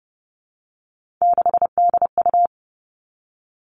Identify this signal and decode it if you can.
Morse code, heard as a keyed tone, about 30 wpm, 700 Hz. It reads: 6DU
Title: Three Challenging Letter-Number Combinations 30wpm